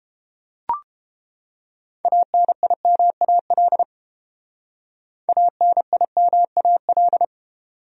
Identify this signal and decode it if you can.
Morse code, heard as a keyed tone, about 30 wpm, 700 Hz. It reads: ANIMAL ANIMAL